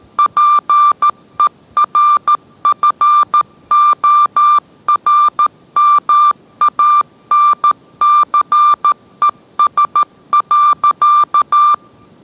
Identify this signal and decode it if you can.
Morse code, heard as a keyed tone, about 13 wpm, 1180 Hz. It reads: PERFORMANCES.